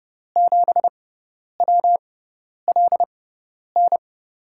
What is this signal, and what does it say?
Morse code, heard as a keyed tone, about 32 wpm, spaced as for 10 wpm, 700 Hz. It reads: 7WLN
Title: USA Call Signs - 6x Character Spacing - 30wpm